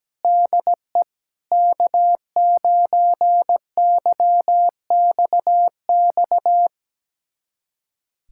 Morse code, heard as a keyed tone, about 17 wpm, 700 Hz. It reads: DE K9YXX
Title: USA Callsign DE Callsign - 5 Characters - 17wpm